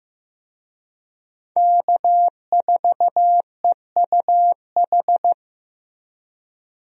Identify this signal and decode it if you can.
Morse code, heard as a keyed tone, about 14 wpm, 700 Hz. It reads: K4EUH
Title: USA Call Signs - 5 Characters - Repeated 3x - 15wpm